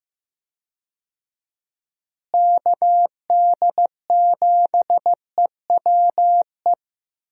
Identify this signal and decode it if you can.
Morse code, heard as a keyed tone, about 15 wpm, 700 Hz. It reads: KD7EWE
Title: USA Call Signs - 6 Characters - Repeated 3x - 15wpm